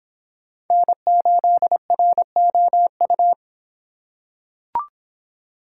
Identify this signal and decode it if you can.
Morse code, heard as a keyed tone, about 24 wpm, 700 Hz. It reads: N8ROU